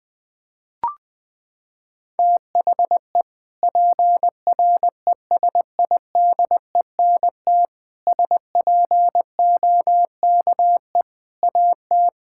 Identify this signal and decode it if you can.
Morse code, heard as a keyed tone, about 20 wpm, 700 Hz. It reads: THE PRESIDENT SPOKE AT
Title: Sentences with 4-Word Phrases - 20wpm